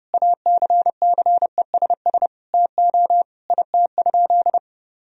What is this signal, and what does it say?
Morse code, heard as a keyed tone, about 32 wpm, 700 Hz. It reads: ACCESS TO IT?
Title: Sentences from Top 1000 Words - Review Part 4 - 30wpm